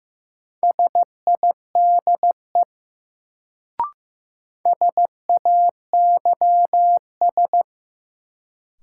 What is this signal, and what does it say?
Morse code, heard as a keyed tone, about 15 wpm, 700 Hz. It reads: SIDE SAYS
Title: Sets of 2 Words - New Word Drill - 15wpm